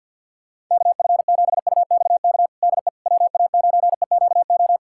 Morse code, heard as a keyed tone, about 52 wpm, 700 Hz. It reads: KF7UXK DE WA9EQO
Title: USA Callsign DE Callsign 50wpm